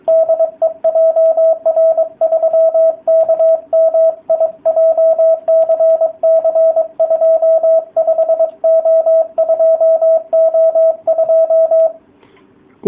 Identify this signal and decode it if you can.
Morse code, heard as a keyed tone, about 23 wpm, 660 Hz, with radio noise. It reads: DE JR3KMI JCC25O2O2